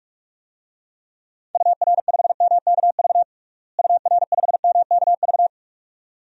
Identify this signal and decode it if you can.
Morse code, heard as a keyed tone, about 46 wpm, 700 Hz. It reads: UR5MKV UR5MKV
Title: DXCC-100 Call Signs - 6 Characters - Repeated 2x - 45wpm